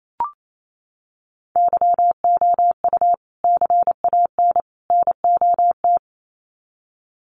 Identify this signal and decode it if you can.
Morse code, heard as a keyed tone, about 28 wpm, 700 Hz. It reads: YOU CAN NOT
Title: Sets of 3 Words 28wpm